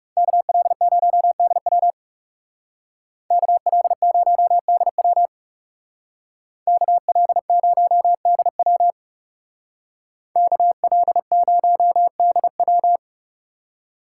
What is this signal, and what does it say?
Morse code, heard as a keyed tone, about 35 wpm, 700 Hz. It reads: KL0DW KL0DW KL0DW KL0DW
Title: Speed Racing - USA Call Signs 30wpm